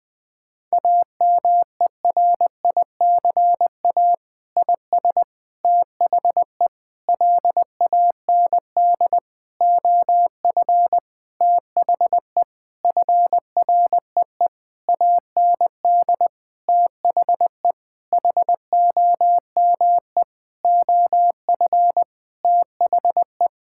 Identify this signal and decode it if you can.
Morse code, heard as a keyed tone, about 20 wpm, 700 Hz. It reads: AMERICA IS THE LAND OF THE FREE AND THE HOME OF THE
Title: Sentences with 8-Word Phrases - 20wpm